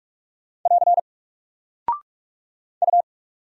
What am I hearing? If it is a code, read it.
Morse code, heard as a keyed tone, about 44 wpm, 700 Hz. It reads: <AR> U
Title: Single Character (Letter:Number:Punctuation:Prosign) 45wpm